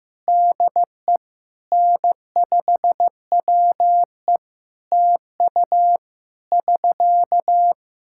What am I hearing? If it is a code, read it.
Morse code, heard as a keyed tone, about 14 wpm, 700 Hz. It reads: DE N5WE TU <SK>
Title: Practice QSO - Exchange 4 - Ending - 15wpm